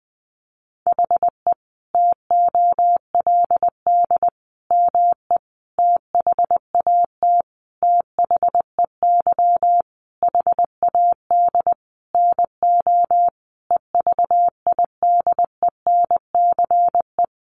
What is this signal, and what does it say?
Morse code, heard as a keyed tone, about 20 wpm, 700 Hz. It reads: HE TOLD ME THAT THEY HAD NO EVIDENCE
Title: Sentences from Top 1300 Words 20wpm